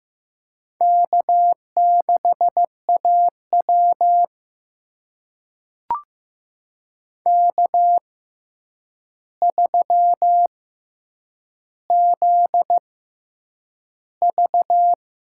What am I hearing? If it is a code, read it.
Morse code, heard as a keyed tone, about 15 wpm, spaced as for 8 wpm, 700 Hz. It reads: K6AW K3ZV
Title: USA Call Signs - 4 Characters - 6x Character Spacing - 15wpm